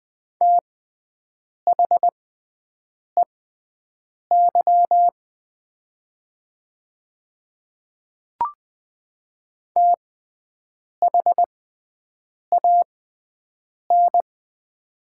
Morse code, heard as a keyed tone, about 20 wpm, spaced as for 7 wpm, 700 Hz. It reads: THEY THAN
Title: 6x Character Spacing - Top 100 Words - 20wpm